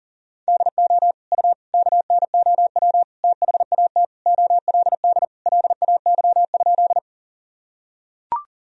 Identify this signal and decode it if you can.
Morse code, heard as a keyed tone, about 40 wpm, 700 Hz. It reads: DO U KNOW THAT OLD LAY?
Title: Sentences from Top 400 Words and Abbreviations 40wpm